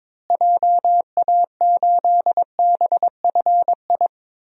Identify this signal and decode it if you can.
Morse code, heard as a keyed tone, about 22 wpm, 700 Hz. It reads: JA8BFI
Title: DXCC-100 Call Signs - 6 Characters - Repeated 3x - 22wpm